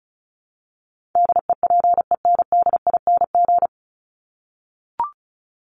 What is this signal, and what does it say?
Morse code, heard as a keyed tone, about 34 wpm, 700 Hz. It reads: DEPENDING
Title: ING Suffix - Top 100 Words with Base Words 35wpm